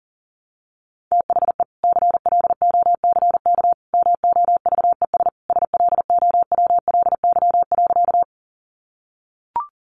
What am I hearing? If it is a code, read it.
Morse code, heard as a keyed tone, about 40 wpm, 700 Hz. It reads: THE CLOCK MOVES SLOWLY.
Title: Easy Phrases 40wpm